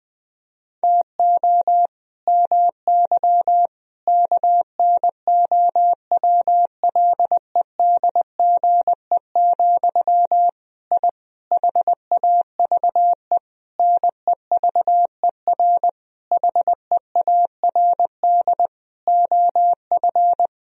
Morse code, heard as a keyed tone, about 20 wpm, 700 Hz. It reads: TO MY KNOWLEDGE, I HAVE NEVER HEARD OF
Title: Sentences from Top 1000 Words - Review Part 3 - 20wpm